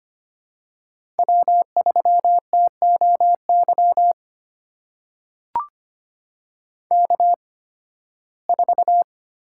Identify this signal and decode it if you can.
Morse code, heard as a keyed tone, about 25 wpm, spaced as for 16 wpm, 700 Hz. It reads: W3TOY K4
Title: USA Call Signs - 5 Characters - 8x Character Spacing - 25wpm